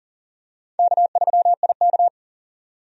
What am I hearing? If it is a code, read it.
Morse code, heard as a keyed tone, about 40 wpm, 700 Hz. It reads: K3IK
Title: USA Call Signs - 4 Characters - 40wpm